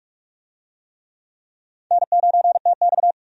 Morse code, heard as a keyed tone, about 46 wpm, 700 Hz. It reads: N9TX
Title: USA Call Signs - 4 Characters - 45wpm